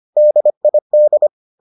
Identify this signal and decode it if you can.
Morse code, heard as a keyed tone, about 25 wpm, 600 Hz. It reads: DID